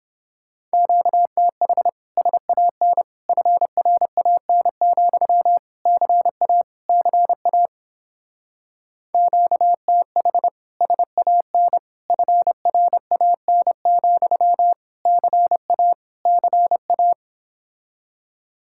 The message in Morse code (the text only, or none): QTH SAN FRAN, CA CA QTH SAN FRAN, CA CA